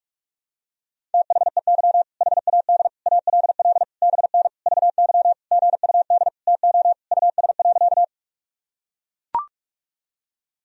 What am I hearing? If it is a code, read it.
Morse code, heard as a keyed tone, about 41 wpm, 700 Hz. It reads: THEY HAD ALL BN VY GUD TO US.
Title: Sentences from Top 200 Words and Abbreviations 45wpm